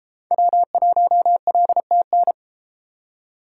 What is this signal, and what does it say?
Morse code, heard as a keyed tone, about 33 wpm, 700 Hz. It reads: W1LTN